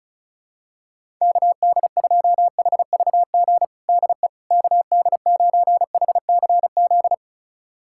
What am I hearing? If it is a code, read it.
Morse code, heard as a keyed tone, about 36 wpm, 700 Hz. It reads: KD2HVG DE KD9HCZ
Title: USA Callsign DE Callsign - 6 Characters - 35wpm